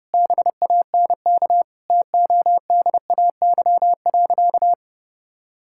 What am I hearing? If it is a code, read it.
Morse code, heard as a keyed tone, about 30 wpm, 700 Hz. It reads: BANK TODAY.
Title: Sentences from Top 1000 Words - Review Part 6 - 30wpm